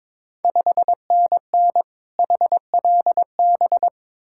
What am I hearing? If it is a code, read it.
Morse code, heard as a keyed tone, about 22 wpm, 690 Hz. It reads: 5NN HLB
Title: Ontario QSO Party Exchange 22wpm